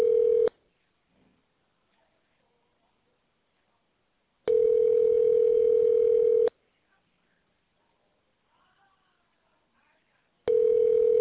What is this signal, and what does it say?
Ringback tone (North America), 440 + 480 Hz.